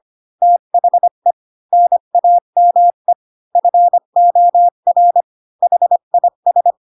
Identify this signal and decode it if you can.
Morse code, heard as a keyed tone, about 25 wpm, 700 Hz. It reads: THE NAME FOR HIS
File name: Pinocchio / Pinocchio ch3 20 25 ik1hge